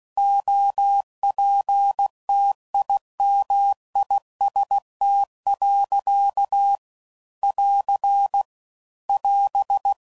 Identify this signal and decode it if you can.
Morse code, heard as a keyed tone, about 16 wpm, 800 Hz. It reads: OPTIMIST. <AR> <AS>
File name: Sayings / Quotes / Optimist